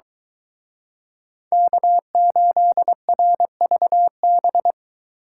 Morse code, heard as a keyed tone, about 23 wpm, 700 Hz. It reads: K8RVB